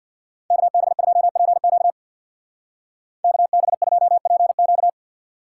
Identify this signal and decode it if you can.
Morse code, heard as a keyed tone, about 58 wpm, 700 Hz. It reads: KB2PQ KB2PQ
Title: Speed Racing - USA 2x2 Call Signs 50wpm